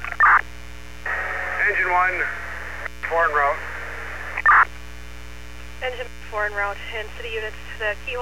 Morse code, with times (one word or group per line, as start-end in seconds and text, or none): none